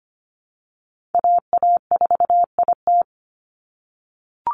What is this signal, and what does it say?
Morse code, heard as a keyed tone, about 25 wpm, 700 Hz. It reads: AA4IT
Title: NA SOTA Chaser Callsigns 25wpm